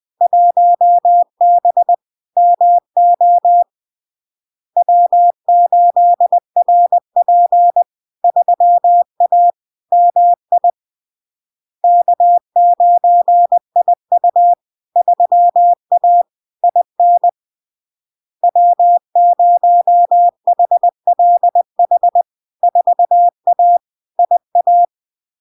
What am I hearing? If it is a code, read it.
Morse code, heard as a keyed tone, about 20 wpm, 700 Hz. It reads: 1B MO W8RP 3A MI K9IU 3A IN W0HLH 4A IA
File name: LARK-FD-2022-Log-in-CW-20wpm-700-Hz